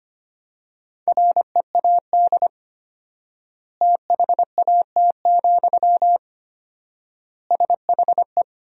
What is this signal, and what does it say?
Morse code, heard as a keyed tone, about 25 wpm, 700 Hz. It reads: READ THAT, SHE
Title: 4x Word-Spacing - Sentences from Top 200 Words - 25wpm